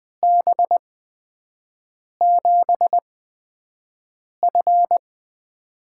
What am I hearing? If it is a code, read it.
Morse code, heard as a keyed tone, about 20 wpm, spaced as for 5 wpm, 700 Hz. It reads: B7F
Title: USA Call Signs - 4 Characters - 8x Character Spacing - 20wpm